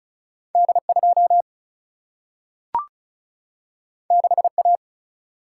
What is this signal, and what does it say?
Morse code, heard as a keyed tone, about 36 wpm, 700 Hz. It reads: D2 6A
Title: Single-Letter Single-Number 35wpm